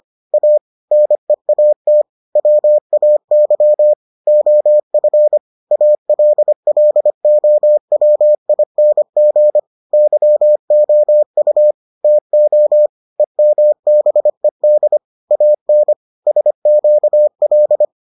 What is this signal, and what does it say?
Morse code, heard as a keyed tone, about 25 wpm, 600 Hz. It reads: A NEAT WAY OF ALLOWING YOU TO EMBED AN SQL